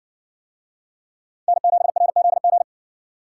Morse code, heard as a keyed tone, about 56 wpm, 700 Hz. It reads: N7RZG